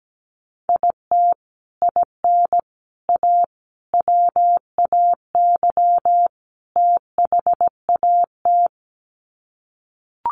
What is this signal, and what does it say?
Morse code, heard as a keyed tone, about 17 wpm, 700 Hz. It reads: IT IN A WAY THAT
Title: Sets of 5 Words Encore2 17wpm